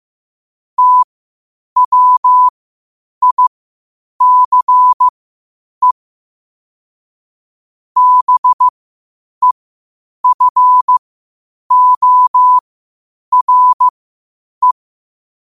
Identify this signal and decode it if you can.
Morse code, heard as a keyed tone, about 15 wpm, spaced as for 8 wpm, 1000 Hz. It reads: TWICE BEFORE
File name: qotdfiles / 8wpm